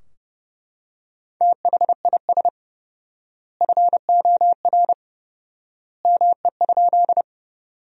Morse code, heard as a keyed tone, about 30 wpm, 700 Hz. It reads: THIS FOR ME?
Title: 4x Word-Spacing - Sentences from Top 200 Words - 30wpm